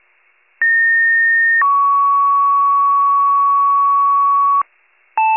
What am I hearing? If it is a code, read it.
Two-tone page, heard as a one-second tone, then a three-second tone, about 1850 then 1120 Hz, with noise. It is heard from 0.6 to 4.6 s.
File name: wcalert